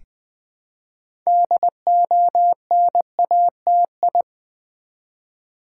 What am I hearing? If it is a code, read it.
Morse code, heard as a keyed tone, about 20 wpm, 700 Hz. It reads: DONATI
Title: Build A Word - 2 Letters At A Time - 8-Letter Words - 20wpm